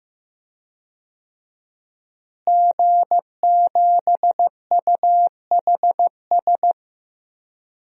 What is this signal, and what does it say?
Morse code, heard as a keyed tone, about 16 wpm, 700 Hz. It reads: G7UHS
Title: DXCC-100 Call Signs - 5 Characters - Repeated 3x - 15wpm